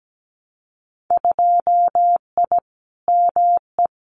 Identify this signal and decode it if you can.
Morse code, heard as a keyed tone, about 17 wpm, 700 Hz. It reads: 2I ME
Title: Winter Field Day Exchanges Encore 17wpm